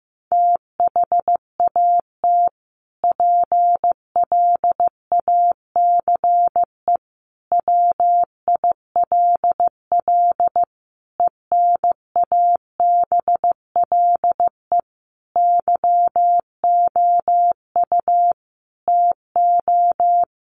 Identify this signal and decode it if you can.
Morse code, heard as a keyed tone, about 15 wpm, 700 Hz. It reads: THAT PLACE WILL ENABLE YOU TO
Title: Sentences from Top 1500 Words 15wpm